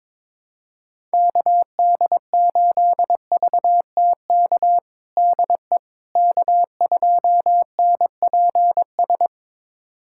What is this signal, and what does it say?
Morse code, heard as a keyed tone, about 22 wpm, 700 Hz. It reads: KD8VTK DE K2NPS
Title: USA Callsign DE Callsign 22wpm